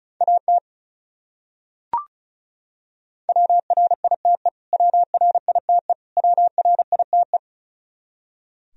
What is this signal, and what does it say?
Morse code, heard as a keyed tone, about 34 wpm, 700 Hz. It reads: AT WRITE WRITE WRITE
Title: Top 100 Words Repeated 35wpm